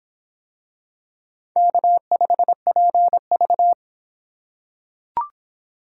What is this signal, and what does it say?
Morse code, heard as a keyed tone, about 23 wpm, 700 Hz. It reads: K5PV